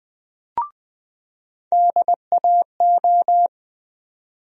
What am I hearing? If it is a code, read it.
Morse code, heard as a keyed tone, about 20 wpm, 700 Hz. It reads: DAO
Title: USA Call Sign Postfixes - 3 Letters 20wpm